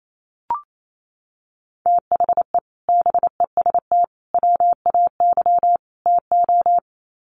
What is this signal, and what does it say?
Morse code, heard as a keyed tone, about 28 wpm, 700 Hz. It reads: THE BEST WAY TO
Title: Sets of 4 Words 28wpm